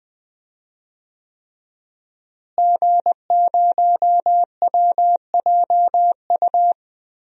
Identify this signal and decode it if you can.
Morse code, heard as a keyed tone, about 20 wpm, 700 Hz. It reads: G0WJU
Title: DXCC-100 Call Signs - 5 Characters - Repeated 3x - 20wpm